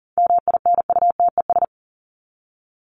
Morse code, heard as a keyed tone, about 40 wpm, 700 Hz. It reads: MINUTES
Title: ES:S Suffix - Top 100 Words with Base Words 40wpm